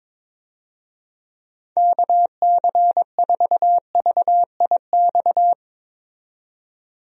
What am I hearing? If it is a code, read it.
Morse code, heard as a keyed tone, about 22 wpm, 700 Hz. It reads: KC4VIX
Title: USA Call Signs - 6 Characters - 22wpm